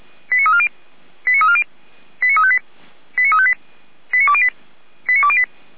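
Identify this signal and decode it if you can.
Five-tone selective call (ZVEI), heard as quick runs of five tones, tones of about 70 ms. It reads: 89340 89340 89358 89358 89208 89208
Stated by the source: ZVEI five-tone selective call received with an SDR.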